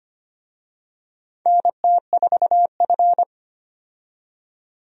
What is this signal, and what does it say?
Morse code, heard as a keyed tone, about 25 wpm, 700 Hz. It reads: NT4F
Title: USA Call Signs - 4 Characters - 25wpm